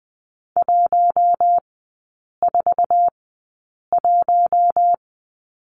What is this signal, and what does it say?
Morse code, heard as a keyed tone, about 20 wpm, spaced as for 8 wpm, 700 Hz. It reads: 141
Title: 047 Lesson 9e - Number 4 - Three Characters 20wpm